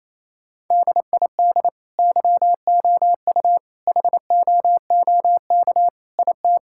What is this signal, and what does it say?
Morse code, heard as a keyed tone, about 28 wpm, 700 Hz. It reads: DID YOU HOOK IT
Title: Sentences from Top 1600 Words 28wpm